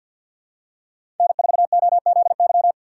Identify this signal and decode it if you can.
Morse code, heard as a keyed tone, about 52 wpm, 700 Hz. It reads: N4OZY
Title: USA Call Signs - 5 Characters - 2x Character Spacing - 50wpm